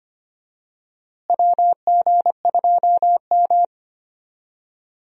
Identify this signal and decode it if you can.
Morse code, heard as a keyed tone, about 25 wpm, 700 Hz. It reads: WG2M
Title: USA Call Signs - 4 Characters - Repeated 2x - 25wpm